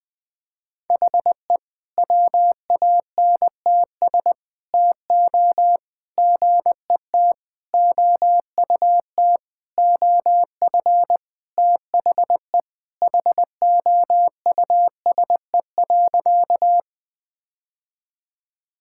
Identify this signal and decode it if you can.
Morse code, heard as a keyed tone, about 20 wpm, 700 Hz. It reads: HE WANTS TO GET OUT OF THE HOUSE.
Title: Sentences with 4-Word Phrases - Spoken Beginning - 20wpm